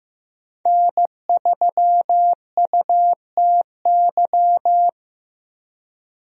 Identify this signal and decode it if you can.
Morse code, heard as a keyed tone, about 15 wpm, 700 Hz. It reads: N3UTY
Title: USA Call Signs - 5 Characters - Repeated 2x - 15wpm